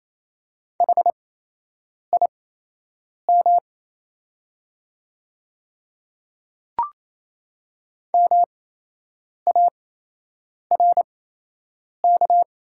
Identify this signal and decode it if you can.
Morse code, heard as a keyed tone, about 28 wpm, spaced as for 8 wpm, 700 Hz. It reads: HIM MARK
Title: 8x Character Spacing - Top 500 Words - 28wpm